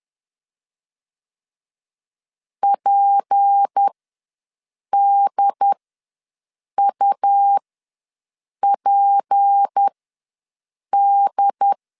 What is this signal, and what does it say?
Morse code, heard as a keyed tone, about 10 wpm, spaced as for 6 wpm, 800 Hz. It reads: PDUPD
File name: LEZIONI / 6 PKX0 / 6 P ws 10wpm 3751